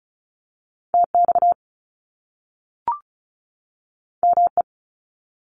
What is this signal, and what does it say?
Morse code, heard as a keyed tone, about 36 wpm, 700 Hz. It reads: TX ME
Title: US States 35wpm